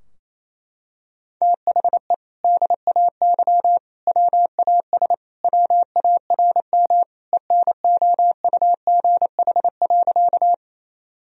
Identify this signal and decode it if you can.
Morse code, heard as a keyed tone, about 27 wpm, 700 Hz. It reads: THE DAY WAS WARM ENOUGH.
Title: Sentences from Top 1000 Words - Review Part 2 - 28wpm